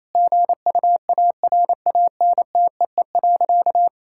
Morse code, heard as a keyed tone, about 28 wpm, 700 Hz. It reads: GUARANTEE.
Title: Sentences from Top 1600 Words 28wpm